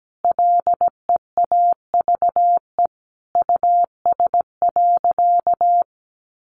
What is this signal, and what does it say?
Morse code, heard as a keyed tone, about 17 wpm, 700 Hz. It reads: LEAVE US.
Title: Sentences from Top 400 Words 17wpm